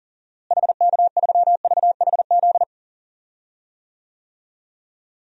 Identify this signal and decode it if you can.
Morse code, heard as a keyed tone, about 40 wpm, 700 Hz. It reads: HK3VHZ
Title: DXCC-100 Call Signs - 6 Characters - Repeated 2x - 40wpm